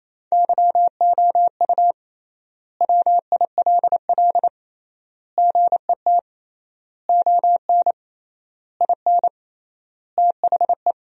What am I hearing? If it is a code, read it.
Morse code, heard as a keyed tone, about 28 wpm, 700 Hz. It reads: YOU WILL GET ON IN THE
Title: 3x Word-Spacing - Sentences from Top 200 Words - 28wpm